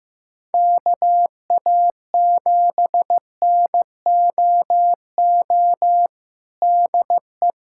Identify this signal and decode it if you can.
Morse code, heard as a keyed tone, about 15 wpm, 700 Hz. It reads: KA7NOO DE
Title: USA Callsign DE Callsign 15wpm